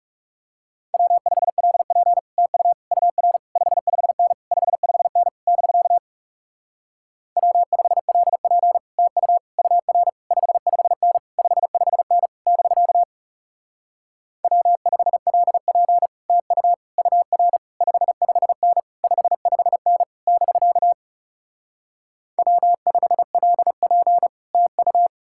Morse code, heard as a keyed tone, about 38 wpm, 700 Hz. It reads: W5LP TU UR 55N 55N <BK> W5LP TU UR 55N 55N <BK> W5LP TU UR 55N 55N <BK> W5LP TU